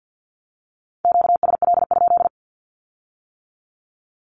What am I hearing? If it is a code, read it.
Morse code, heard as a keyed tone, about 50 wpm, 700 Hz. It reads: QSL?
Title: Q-Codes with Spoken Meaning - 50wpm